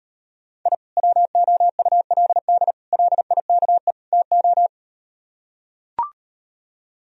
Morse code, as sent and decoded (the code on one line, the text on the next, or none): .. / .-- --- ..- .-.. -.. / .-.. .. -.- . / - ---
I WOULD LIKE TO